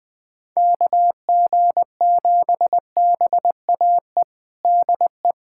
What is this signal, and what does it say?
Morse code, heard as a keyed tone, about 20 wpm, 700 Hz. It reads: KG7BAE DE
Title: USA Callsign DE Callsign - 6 Characters - 20wpm